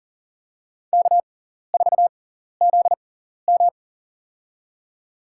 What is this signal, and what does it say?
Morse code, heard as a keyed tone, about 40 wpm, spaced as for 14 wpm, 700 Hz. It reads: K4ZM
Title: USA Call Signs - 6x Character Spacing - 40wpm